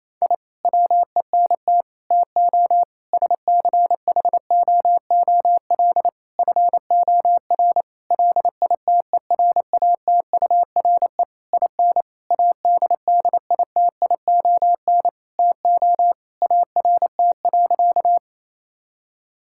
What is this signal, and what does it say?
Morse code, heard as a keyed tone, about 27 wpm, 700 Hz. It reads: I WENT TO SCHOOL FOR LITERATURE IN ADDITION TO ART.
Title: Sentences from Top 1000 Words - Review Part 6 - 28wpm